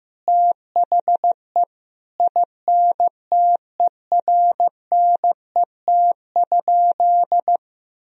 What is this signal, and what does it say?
Morse code, heard as a keyed tone, about 15 wpm, 700 Hz. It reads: THE INTERNET?
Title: Sentences from Top 1000 Words - Review Part 4 - 15wpm